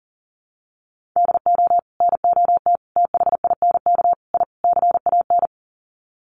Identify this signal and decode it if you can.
Morse code, heard as a keyed tone, about 40 wpm, 700 Hz. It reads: DO NOT THINK I CAN
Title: Sets of 5 Words Encore2 40wpm